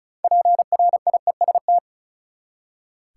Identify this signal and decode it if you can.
Morse code, heard as a keyed tone, about 36 wpm, 700 Hz. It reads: PRIEST
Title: Six Letter Words - 35wpm